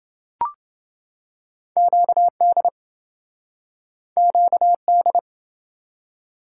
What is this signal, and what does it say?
Morse code, heard as a keyed tone, about 28 wpm, 700 Hz. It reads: QD QD